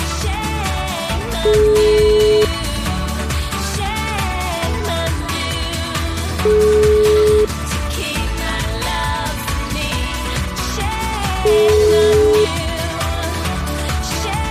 Ringback tone (Europe, 425 Hz), with noise.